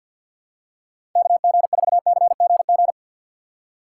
Morse code, heard as a keyed tone, about 53 wpm, 690 Hz. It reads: KG4CGG
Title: USA Call Signs - 6x Character Spacing - 50wpm